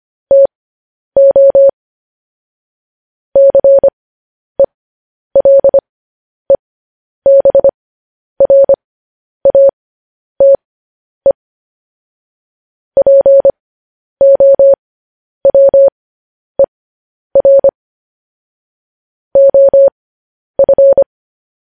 Morse code, heard as a keyed tone, about 25 wpm, spaced as for 10 wpm, 550 Hz. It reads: TO CELEBRATE POWER OF
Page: news-headlines-in-morse-code-at-10-wpm